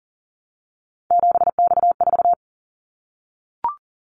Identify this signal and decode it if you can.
Morse code, heard as a keyed tone, about 42 wpm, 700 Hz. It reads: 7X4
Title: Callsign Prefixes for Top 100 DXCC - 40wpm